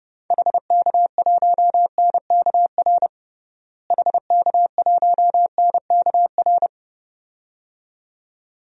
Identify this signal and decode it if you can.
Morse code, heard as a keyed tone, about 32 wpm, 700 Hz. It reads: HK1NKR HK1NKR
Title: DXCC-100 Call Signs - 6 Characters - Repeated 2x - 30wpm